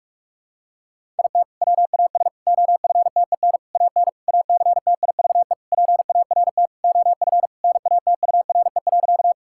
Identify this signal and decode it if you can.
Morse code, heard as a keyed tone, about 45 wpm, 700 Hz. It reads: IT WAS OFTEN AN ACTIVE PART OF NATURE.
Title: Sentences from Top 1000 Words - Review Part 6 - 45wpm